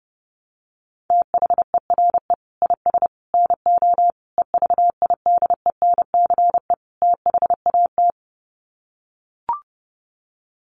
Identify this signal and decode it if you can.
Morse code, heard as a keyed tone, about 30 wpm, 700 Hz. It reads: THERE IS NO EVIDENCE THAT
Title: Sets of 5 Words Encore 30wpm